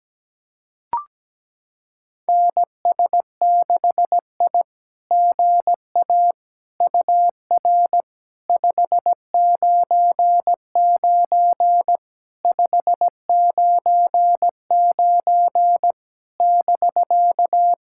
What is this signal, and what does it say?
Morse code, heard as a keyed tone, about 17 wpm, 700 Hz. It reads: NS6I GA UR 599 599 <BK>
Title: SOTA QSO Practice 17wpm